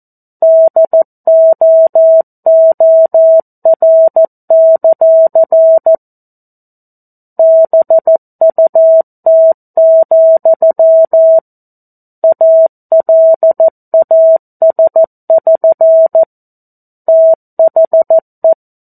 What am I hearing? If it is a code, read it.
Morse code, heard as a keyed tone, about 14 wpm, 650 Hz. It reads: DOOR; BUT, ALAS<SN> THE
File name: 02-Chapter-02 14 WPM